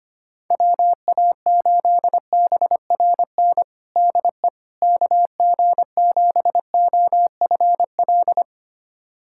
Morse code, heard as a keyed tone, about 25 wpm, 700 Hz. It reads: WA8BRN DE KG7OFL
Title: USA Callsign DE Callsign - 6 Characters - 25wpm